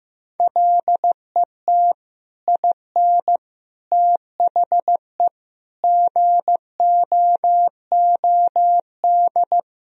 Morse code, heard as a keyed tone, about 15 wpm, 700 Hz. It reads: LET IN THE GOOD
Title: Sentences from Top 1000 Words - Review Part 1 - 15wpm